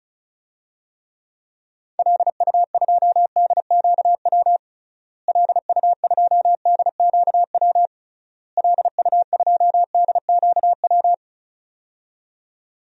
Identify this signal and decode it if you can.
Morse code, heard as a keyed tone, about 36 wpm, 700 Hz. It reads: LU2DQW LU2DQW LU2DQW
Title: DXCC-100 Call Signs - 6 Characters - Repeated 3x - 35wpm